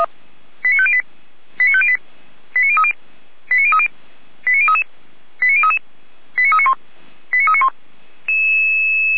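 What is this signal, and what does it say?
Five-tone selective call (ZVEI), heard as quick runs of five tones, tones of about 70 ms. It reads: 89508 89508 89030 89030 89033 89033 89391 89391
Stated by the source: ZVEI five-tone selective call received on an SDR.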